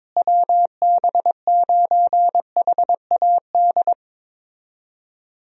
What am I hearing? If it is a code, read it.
Morse code, heard as a keyed tone, about 22 wpm, 700 Hz. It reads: WB9HAD
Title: USA Call Signs - 6 Characters - Repeated 3x - 22wpm